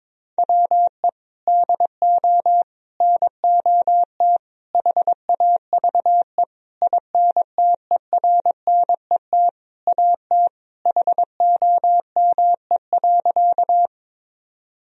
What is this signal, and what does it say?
Morse code, heard as a keyed tone, about 21 wpm, 700 Hz. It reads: WE DO NOT HAVE INTERNET AT HOME.
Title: Sentences from Top 1000 Words - Review Part 4 - 22wpm